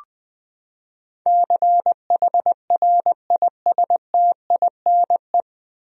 Morse code, heard as a keyed tone, about 21 wpm, 700 Hz. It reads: CHRISTINE
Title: USA Common Names - 20wpm